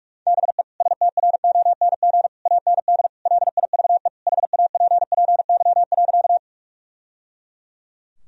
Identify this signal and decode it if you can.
Morse code, heard as a keyed tone, about 42 wpm, 700 Hz. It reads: BE STRONG AND LIVE HAPPY.
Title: Sentences from Top 1000 Words - Review Part 3 - 45wpm